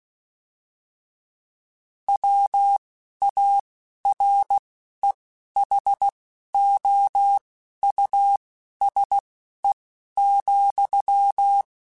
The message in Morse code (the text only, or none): WAREHOUSE,